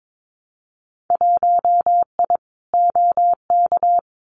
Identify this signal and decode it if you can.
Morse code, heard as a keyed tone, about 22 wpm, 700 Hz. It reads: 1I OK
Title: Winter Field Day Exchanges Encore 22wpm